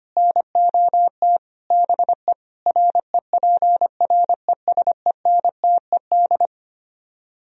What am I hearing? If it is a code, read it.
Morse code, heard as a keyed tone, about 24 wpm, 700 Hz. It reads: NOT BE REPRESENTED
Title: Speed-Racing - Sets of 5 Words 25wpm